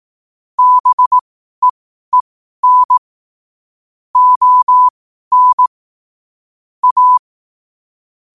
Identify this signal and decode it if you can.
Morse code, heard as a keyed tone, about 18 wpm, spaced as for 12 wpm, 1000 Hz. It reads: BEEN ON A
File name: qotdfiles / 12wpm